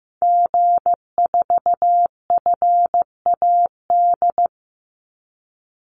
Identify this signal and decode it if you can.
Morse code, heard as a keyed tone, about 15 wpm, 700 Hz. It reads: G4FAD
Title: Call Signs Repeated 2x - 15wpm